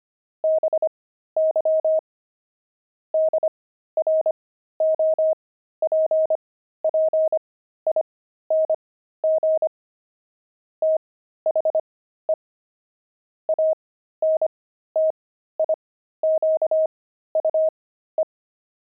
Morse code, heard as a keyed tone, about 25 wpm, spaced as for 13 wpm, 630 Hz. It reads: BY DROPPING THE ANTIQUE
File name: pangram-fs-13-cs-25-antique